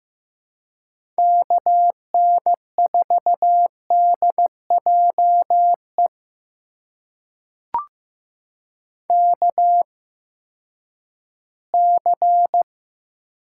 Morse code, heard as a keyed tone, about 15 wpm, spaced as for 10 wpm, 700 Hz. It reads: KN4DJE KC
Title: USA Call Signs - 6 Characters - 8x Character Spacing - 15wpm